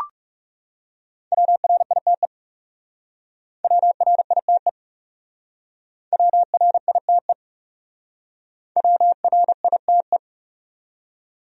Morse code, heard as a keyed tone, about 40 wpm, 700 Hz. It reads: WRITE WRITE WRITE WRITE